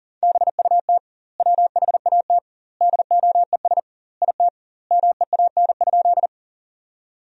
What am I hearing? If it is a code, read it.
Morse code, heard as a keyed tone, about 40 wpm, 700 Hz. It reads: BUT WHAT DOES IT MEAN?
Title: 2x Word-Spacing - Sentences from Top 200 Words - 40wpm